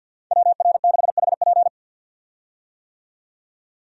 Morse code, heard as a keyed tone, about 50 wpm, 700 Hz. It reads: WR6HP
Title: USA Call Signs - 5 Characters - 50wpm